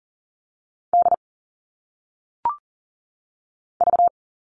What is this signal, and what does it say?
Morse code, heard as a keyed tone, about 39 wpm, 700 Hz. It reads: D V
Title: All Cut Numbers - 1-Digit 40wpm